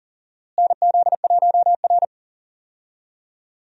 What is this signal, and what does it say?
Morse code, heard as a keyed tone, about 40 wpm, 700 Hz. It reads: NZ1R
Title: USA Call Signs - 4 Characters - Repeated 3x - 40wpm